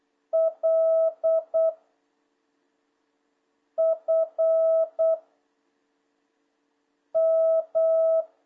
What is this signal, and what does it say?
Morse code, heard as a keyed tone, about 8 wpm, 650 Hz. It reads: L F M
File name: jh8cbh / 2026cw